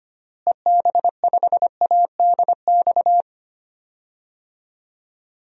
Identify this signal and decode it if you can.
Morse code, heard as a keyed tone, about 25 wpm, 700 Hz. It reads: EB5ADX
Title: DXCC-100 Call Signs - 6 Characters - Repeated 2x - 25wpm